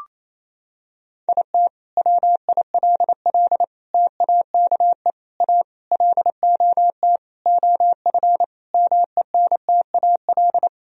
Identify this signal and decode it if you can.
Morse code, heard as a keyed tone, about 28 wpm, 700 Hz. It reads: IT WILL TAKE A LOT OF MENTAL
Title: Sentences from Top 1000 Words - Review Part 4 - 28wpm